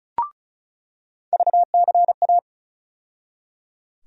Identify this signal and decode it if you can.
Morse code, heard as a keyed tone, about 34 wpm, 700 Hz. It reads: VCA
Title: USA Call Sign Postfixes - 3 Letters 35wpm